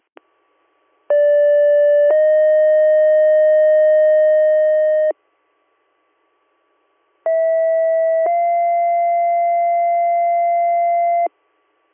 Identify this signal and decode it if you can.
Two-tone page, heard as a one-second tone, then a three-second tone, about 580 then 620 Hz, about 650 then 690 Hz. It is heard from 1.1 to 11.3 s.